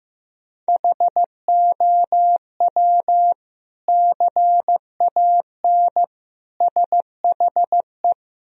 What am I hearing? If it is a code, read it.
Morse code, heard as a keyed tone, about 15 wpm, 700 Hz. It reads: HOW CAN SHE
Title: Sentences from Top 1000 Words - Review Part 2 - 15wpm